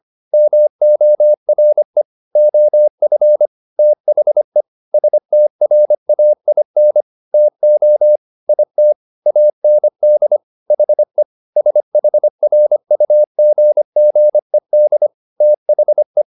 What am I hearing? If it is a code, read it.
Morse code, heard as a keyed tone, about 25 wpm, 600 Hz. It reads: MORE OF THE STRAIN TO IT AND HE SHRUGGED THE
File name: the old man and the sea part 1 25wpm